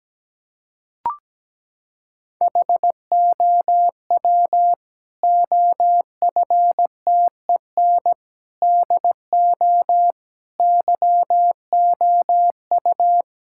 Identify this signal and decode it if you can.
Morse code, heard as a keyed tone, about 17 wpm, 700 Hz. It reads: HOW OFTEN DO YOU
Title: Sentences from Top 1000 Words - Review Part 5 - 17wpm